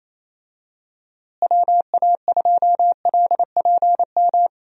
Morse code, heard as a keyed tone, about 28 wpm, 700 Hz. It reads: WA2LPM
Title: USA Call Signs - 6 Characters - Repeated 3x - 28wpm